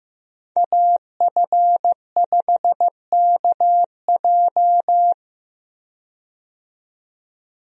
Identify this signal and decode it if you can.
Morse code, heard as a keyed tone, about 15 wpm, 700 Hz. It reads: AF5KJ
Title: USA Call Signs - 5 Characters - 15wpm